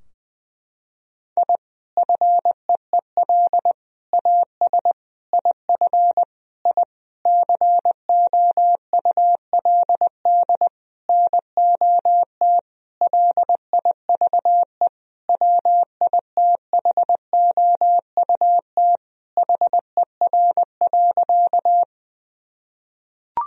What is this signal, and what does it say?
Morse code, heard as a keyed tone, about 21 wpm, 700 Hz. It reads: I FEEL AS IF I COULD NOT LIVE WITHOUT HER.
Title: Sentences from Top 1000 Words - Review Part 3 - 20wpm